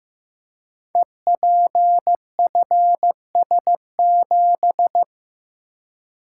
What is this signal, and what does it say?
Morse code, heard as a keyed tone, about 15 wpm, 700 Hz. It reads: EPFS7
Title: Code Groups - 5 Letter:Numbers - 15wpm